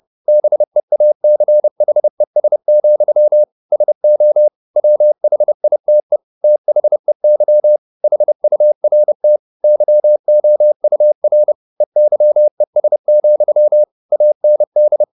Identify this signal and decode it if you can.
Morse code, heard as a keyed tone, about 30 wpm, 600 Hz. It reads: BEACHES, SO WHITE THEY HURT YOUR EYES, AND
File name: the old man and the sea / the old man and the sea 1 30wpm